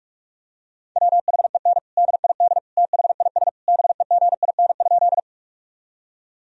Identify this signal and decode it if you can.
Morse code, heard as a keyed tone, about 47 wpm, 700 Hz. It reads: WHEN DID THIS BEGIN?
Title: Sentences from Top 1000 Words - Review Part 1 - 45wpm